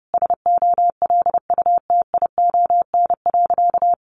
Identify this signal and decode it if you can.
Morse code, heard as a keyed tone, about 30 wpm, 700 Hz. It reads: SOLUTION.
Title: Sentences from Top 1300 Words 30wpm